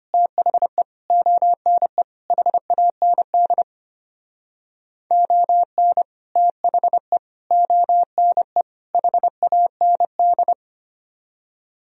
Morse code, heard as a keyed tone, about 32 wpm, 700 Hz. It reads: THE ONE HAND ON THE ONE HAND